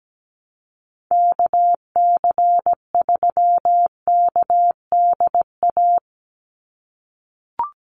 Morse code, heard as a keyed tone, about 17 wpm, 700 Hz. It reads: KC3KDA
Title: USA Call Signs 17wpm